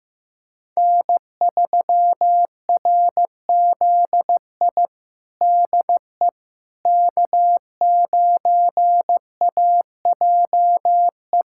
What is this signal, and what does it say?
Morse code, heard as a keyed tone, about 15 wpm, 700 Hz. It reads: N3RZI DE K9AJE
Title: USA Callsign DE Callsign - 5 Characters - 15wpm